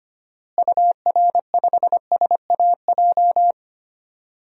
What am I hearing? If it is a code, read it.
Morse code, heard as a keyed tone, about 25 wpm, 700 Hz. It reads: UR5SAJ
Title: DXCC-100 Call Signs - 6 Characters - Repeated 3x - 25wpm